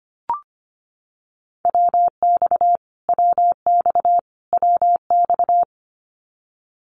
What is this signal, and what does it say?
Morse code, heard as a keyed tone, about 25 wpm, 700 Hz. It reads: WX WX WX
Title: Abbreviations Repeated 3x - 25wpm